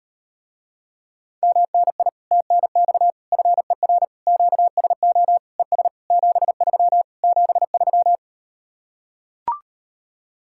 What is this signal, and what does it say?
Morse code, heard as a keyed tone, about 38 wpm, 700 Hz. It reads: MNI TNX FER QSO ES 73 73